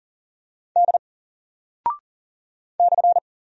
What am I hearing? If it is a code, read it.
Morse code, heard as a keyed tone, about 40 wpm, 700 Hz. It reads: D /
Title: Single Character (Letter:Number:Punctuation:Prosign) 40wpm